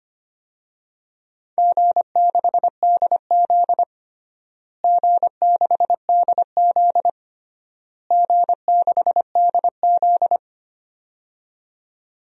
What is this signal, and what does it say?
Morse code, heard as a keyed tone, about 25 wpm, 700 Hz. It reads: G6DZ G6DZ G6DZ
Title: DXCC-100 Call Signs - 4 Characters - Repeated 3x - 25wpm